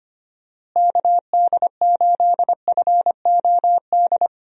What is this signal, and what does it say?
Morse code, heard as a keyed tone, about 25 wpm, 700 Hz. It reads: KD8FOD
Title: USA Call Signs - 6 Characters - Repeated 2x - 25wpm